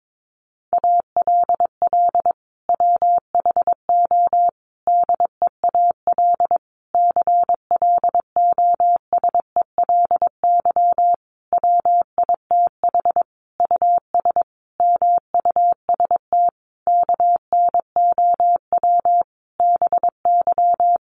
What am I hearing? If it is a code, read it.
Morse code, heard as a keyed tone, about 22 wpm, 700 Hz. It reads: ALL WHO DEAL CLOSELY WITH US MUST KNOW BY
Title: Sentences from Top 1400 Words 22wpm